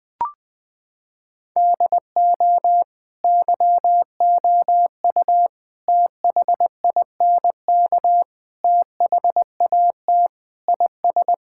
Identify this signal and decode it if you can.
Morse code, heard as a keyed tone, about 21 wpm, 700 Hz. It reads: DO YOU THINK THAT IS
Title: Sentences with 4-Word Phrases - 20wpm